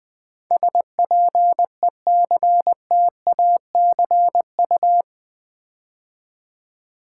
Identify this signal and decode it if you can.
Morse code, heard as a keyed tone, about 20 wpm, 700 Hz. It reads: SPECTACU
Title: Build A Word - 4 Letters At A Time - 11-Letter Words - 20wpm